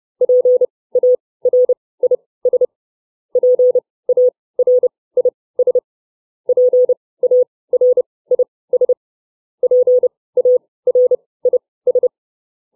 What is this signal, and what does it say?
Morse code, heard as a keyed tone, about 30 wpm, spaced as for 19 wpm, 500 Hz. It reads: PARIS PARIS PARIS PARIS
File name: CW / farnsworth timing 2